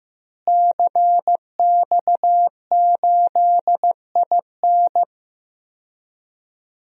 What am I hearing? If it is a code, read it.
Morse code, heard as a keyed tone, about 15 wpm, 700 Hz. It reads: CX8IN
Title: DXCC-100 Call Signs - 5 Characters - Repeated 2x - 15wpm